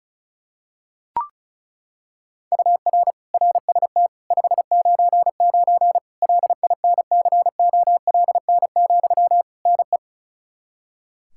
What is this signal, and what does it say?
Morse code, heard as a keyed tone, about 34 wpm, 700 Hz. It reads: UR RST 599 LINCOLN, NE
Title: Practice QSO - Exchange 1 - Beginning - 35wpm